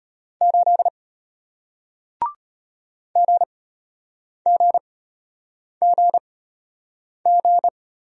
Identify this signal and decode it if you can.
Morse code, heard as a keyed tone, about 33 wpm, spaced as for 8 wpm, 700 Hz. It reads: 8 GGGG